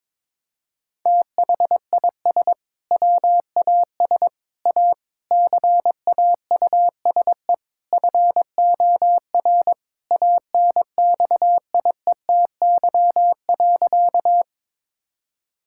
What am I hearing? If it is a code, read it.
Morse code, heard as a keyed tone, about 21 wpm, 700 Hz. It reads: THIS WAS A CAUSE FOR ANXIETY.
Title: Sentences from Top 1600 Words 22wpm